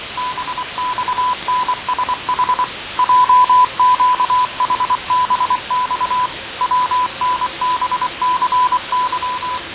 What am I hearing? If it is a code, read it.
Morse code, heard as a keyed tone, about 24 wpm, 1000 Hz, with noise. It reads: BXNSH JQHBX WNBCY
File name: 6824usb